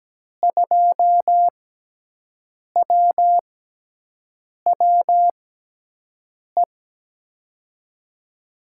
Morse code, heard as a keyed tone, about 17 wpm, 700 Hz. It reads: 2 W W E
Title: USA Call Signs - 6x Character Spacing - 17wpm